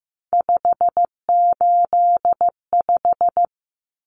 Morse code, heard as a keyed tone, about 15 wpm, 700 Hz. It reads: 585
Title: Common Cut Numbers - 3-Digit 15wpm